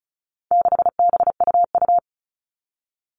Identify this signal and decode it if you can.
Morse code, heard as a keyed tone, about 34 wpm, 700 Hz. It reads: 6BUU
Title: All Cut Numbers - 4-Digit 35wpm